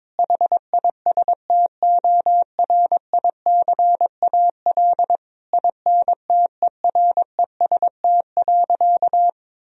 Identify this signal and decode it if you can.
Morse code, heard as a keyed tone, about 22 wpm, 700 Hz. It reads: HISTORICAL INTEREST.
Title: Sentences from Top 1000 Words - Review Part 4 - 22wpm